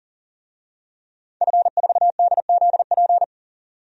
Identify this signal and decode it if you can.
Morse code, heard as a keyed tone, about 40 wpm, 700 Hz. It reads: F4DZP
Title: DXCC-100 Call Signs - 5 Characters - 40wpm